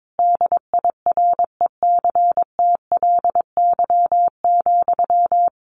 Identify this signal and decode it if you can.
Morse code, heard as a keyed tone, about 22 wpm, 700 Hz. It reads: DIRECTLY,
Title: Sentences from Top 800 Words 22wpm